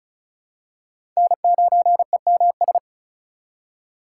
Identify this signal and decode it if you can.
Morse code, heard as a keyed tone, about 34 wpm, 700 Hz. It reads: N9EMS
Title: USA Call Signs - 5 Characters - 4x Character Spacing - 35wpm